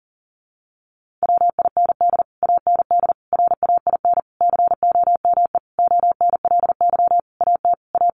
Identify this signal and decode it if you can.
Morse code, heard as a keyed tone, about 37 wpm, 700 Hz. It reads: WIND AND RAIN COME ONLY AT A
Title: Sentences from Top 1400 Words 40wpm